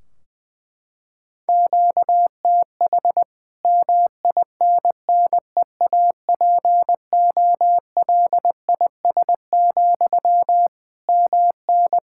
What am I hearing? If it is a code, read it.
Morse code, heard as a keyed tone, about 21 wpm, 700 Hz. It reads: QTH MINNEAPOLIS, MN
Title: Practice QSO - Exchange 1 - Beginning - 20wpm